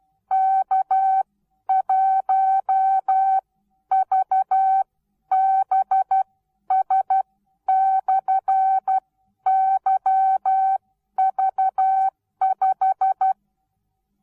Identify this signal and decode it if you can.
Morse code, heard as a keyed tone, about 12 wpm, spaced as for 10 wpm, 770 Hz. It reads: K1VBS/YV5
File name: Onlinelehrgang / morsen / DJ4UF Morsen19